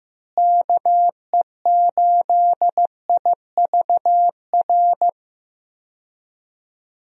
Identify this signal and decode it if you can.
Morse code, heard as a keyed tone, about 15 wpm, 700 Hz. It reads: KE8IVR
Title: USA Call Signs - 6 Characters - Repeated 2x - 15wpm